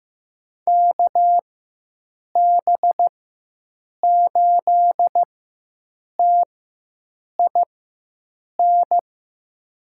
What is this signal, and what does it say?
Morse code, heard as a keyed tone, about 15 wpm, spaced as for 7 wpm, 700 Hz. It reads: KB8TIN
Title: USA Call Signs - 6 Characters - 4x Character Spacing - 15wpm